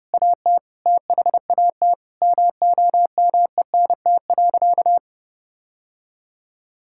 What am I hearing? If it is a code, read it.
Morse code, heard as a keyed tone, about 30 wpm, 700 Hz. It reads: AT THAT MOMENT.
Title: Sentences from Top 1000 Words - Review Part 6 - 30wpm